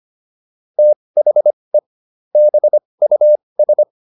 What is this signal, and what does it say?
Morse code, heard as a keyed tone, about 25 wpm, spaced as for 20 wpm, 600 Hz. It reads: THE BUS